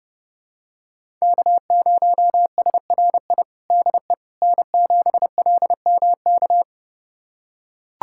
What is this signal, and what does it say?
Morse code, heard as a keyed tone, about 32 wpm, 700 Hz. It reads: K0SRI DE N7LMK
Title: USA Callsign DE Callsign - 5 Characters - 30wpm